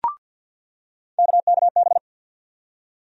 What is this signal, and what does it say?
Morse code, heard as a keyed tone, about 46 wpm, 700 Hz. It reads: KKB
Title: USA Call Sign Postfixes - 3 Letters 50wpm